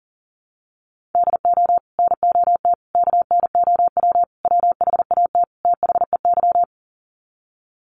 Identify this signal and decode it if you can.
Morse code, heard as a keyed tone, about 37 wpm, 700 Hz. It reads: DO NOT KNOW WHAT THEY
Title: Sets of 5 Words 40wpm